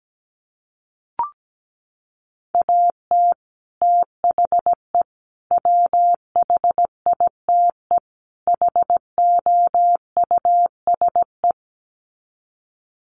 Morse code, heard as a keyed tone, about 17 wpm, 700 Hz. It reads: AT THE WHITE HOUSE
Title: Sets of 4 Words 17wpm